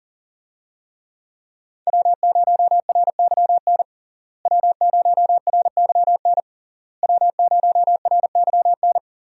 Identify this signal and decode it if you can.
Morse code, heard as a keyed tone, about 40 wpm, 700 Hz. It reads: W0RYN W0RYN W0RYN
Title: USA Call Signs - 5 Characters - Repeated 3x - 40wpm